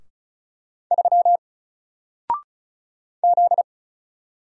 Morse code, heard as a keyed tone, about 34 wpm, 700 Hz. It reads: 3 Z
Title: Single Letter-Number 35wpm